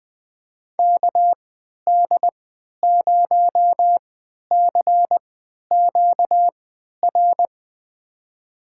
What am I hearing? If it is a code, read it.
Morse code, heard as a keyed tone, about 20 wpm, spaced as for 11 wpm, 700 Hz. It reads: KD0CQR
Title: USA Call Signs - 3x Character Spacing - 20wpm